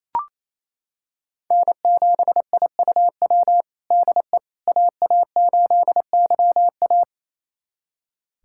Morse code, heard as a keyed tone, about 27 wpm, 700 Hz. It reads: N7IUW DE AA8YA
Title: USA Callsign DE Callsign - 5 Characters - 28wpm